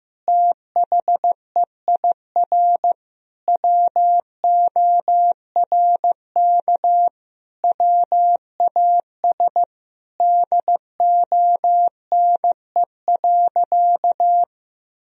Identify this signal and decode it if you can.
Morse code, heard as a keyed tone, about 15 wpm, 700 Hz. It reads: THEIR WORK WAS DONE.
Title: Sentences from Top 1000 Words - Review Part 1 - 15wpm